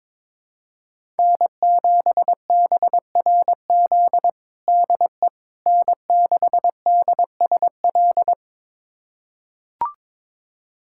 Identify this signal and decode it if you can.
Morse code, heard as a keyed tone, about 22 wpm, 700 Hz. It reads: N7BRZ DE N6DSL
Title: USA Callsign DE Callsign - 5 Characters - 22wpm